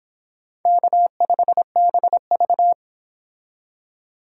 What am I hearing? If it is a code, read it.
Morse code, heard as a keyed tone, about 25 wpm, 700 Hz. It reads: K5BV